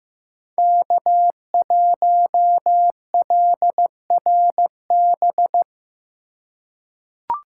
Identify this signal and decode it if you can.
Morse code, heard as a keyed tone, about 16 wpm, 700 Hz. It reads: K1LRB
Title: USA Call Signs - 5 Characters - 2x Character Spacing - 15wpm